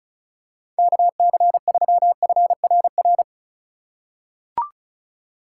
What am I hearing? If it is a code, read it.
Morse code, heard as a keyed tone, about 34 wpm, 700 Hz. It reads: KC3FRR
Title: USA Call Signs - 6 Characters - 2x Character Spacing - 35wpm